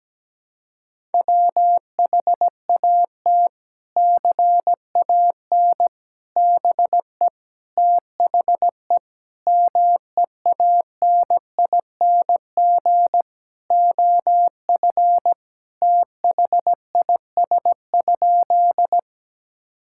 Morse code, heard as a keyed tone, about 17 wpm, 700 Hz. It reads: WHAT CAN BE THE MEANING OF THIS?
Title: Sentences from Top 1000 Words - Review Part 6 - 17wpm